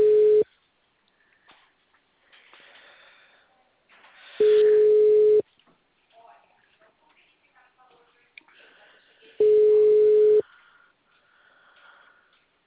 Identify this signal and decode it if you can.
Ringback tone (Europe, 425 Hz), with speech.